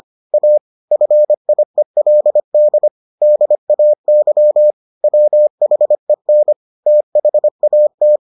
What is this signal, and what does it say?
Morse code, heard as a keyed tone, about 25 wpm, 600 Hz. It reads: A FIELD DAY WHEN THAT